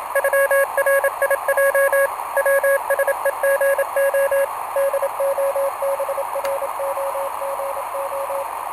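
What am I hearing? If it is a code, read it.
Morse code (Russian alphabet), heard as a keyed tone, about 27 wpm, 550 Hz, with radio noise. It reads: ЮРИЙ ВСЕГО ДОБРОГО
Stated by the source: Portable QRP contact on a home-made 3 watt transmitter with a long dipole, received by a station running home-made 1.2 watts, with key clicks and hard keying in question.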